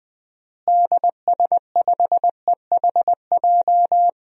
Morse code, heard as a keyed tone, about 21 wpm, 700 Hz. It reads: DS5EHJ
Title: DXCC-100 Call Signs - 6 Characters - Repeated 2x - 20wpm